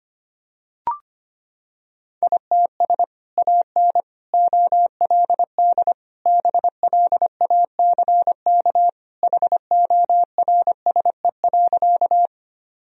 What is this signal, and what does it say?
Morse code, heard as a keyed tone, about 25 wpm, 700 Hz. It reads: ITS AN OLD BLACK HORSE.
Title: Sentences from Top 1000 Words - Review Part 2 - 25wpm